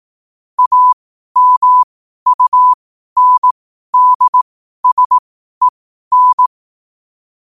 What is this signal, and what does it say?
Morse code, heard as a keyed tone, about 18 wpm, spaced as for 12 wpm, 1000 Hz. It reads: AMUNDSEN
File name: qotdfiles / 12wpm